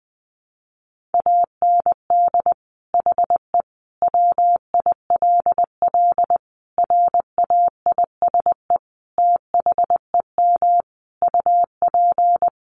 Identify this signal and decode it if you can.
Morse code, heard as a keyed tone, about 20 wpm, 700 Hz. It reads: AND HE WILL RAISE THEM UP
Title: Sentences from Top 1400 Words 20wpm